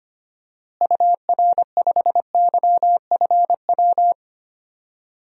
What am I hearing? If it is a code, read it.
Morse code, heard as a keyed tone, about 25 wpm, 700 Hz. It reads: UR5YFW
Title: DXCC-100 Call Signs - 6 Characters - Repeated 2x - 25wpm